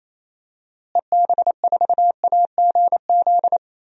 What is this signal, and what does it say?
Morse code, heard as a keyed tone, about 28 wpm, 700 Hz. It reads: EB4AGZ
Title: DXCC-100 Call Signs - 6 Characters - Repeated 3x - 28wpm